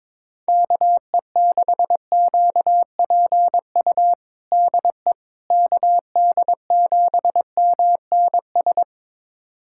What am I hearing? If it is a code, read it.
Morse code, heard as a keyed tone, about 22 wpm, 700 Hz. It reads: KE6QPU DE KD7MNS
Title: USA Callsign DE Callsign 22wpm